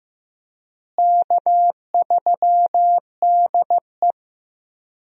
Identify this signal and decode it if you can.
Morse code, heard as a keyed tone, about 15 wpm, 700 Hz. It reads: K3DE
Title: USA Call Signs - 4 Characters - Repeated 2x - 15wpm